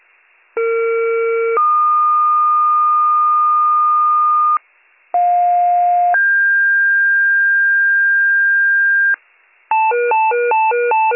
Two-tone page, heard as a one-second tone, then a three-second tone, about 460 then 1220 Hz, about 710 then 1600 Hz, with noise. It is heard from 0.6 to 9.1 s.